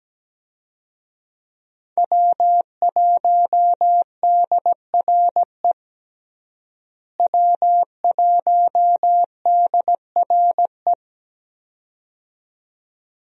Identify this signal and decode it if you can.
Morse code, heard as a keyed tone, about 17 wpm, 700 Hz. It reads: W1DRE W1DRE
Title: USA Call Signs - 5 Characters - Repeated 2x - 17wpm